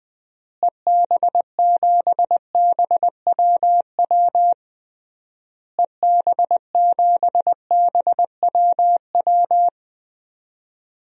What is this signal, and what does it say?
Morse code, heard as a keyed tone, about 20 wpm, 700 Hz. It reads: EB7BWW EB7BWW
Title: DXCC-100 Call Signs - 6 Characters - Repeated 3x - 20wpm